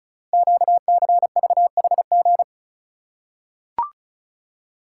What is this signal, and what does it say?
Morse code, heard as a keyed tone, about 34 wpm, 700 Hz. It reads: QCVHG
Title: Code Groups - 5 Letters - 35wpm